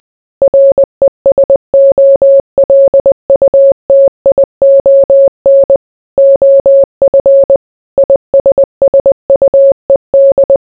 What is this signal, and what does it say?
Morse code, heard as a keyed tone, about 20 wpm, 550 Hz. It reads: RESOLUTION OF ISSUED